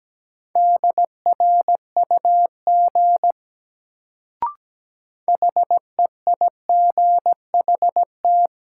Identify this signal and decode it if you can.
Morse code, heard as a keyed tone, about 17 wpm, 700 Hz. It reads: DRUG HEIGHT
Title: Sets of 5 Words - New Word Drill - 17wpm